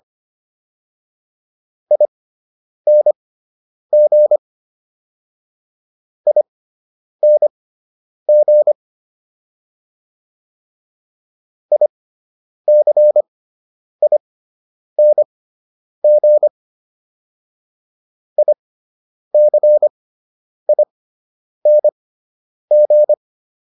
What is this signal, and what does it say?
Morse code, heard as a keyed tone, about 25 wpm, spaced as for 9 wpm, 600 Hz. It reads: ING ING ICING ICING
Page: /fundamental-practice-files/